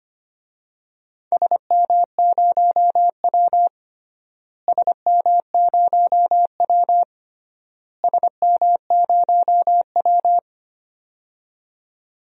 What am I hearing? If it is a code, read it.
Morse code, heard as a keyed tone, about 25 wpm, 700 Hz. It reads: SM0W SM0W SM0W
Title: DXCC-100 Call Signs - 4 Characters - Repeated 3x - 25wpm